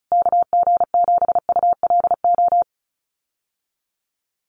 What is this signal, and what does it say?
Morse code, heard as a keyed tone, about 34 wpm, 700 Hz. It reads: KG7ULO
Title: USA Call Signs Repeated 3x 35wpm